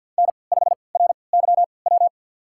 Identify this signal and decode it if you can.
Morse code, heard as a keyed tone, about 51 wpm, spaced as for 28 wpm, 700 Hz. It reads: N5RYW
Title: USA Call Signs - 3x Character Spacing - 50wpm